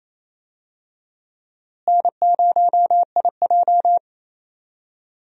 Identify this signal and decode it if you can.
Morse code, heard as a keyed tone, about 28 wpm, 700 Hz. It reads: N0IJ
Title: USA Call Signs - 4 Characters - Repeated 3x - 28wpm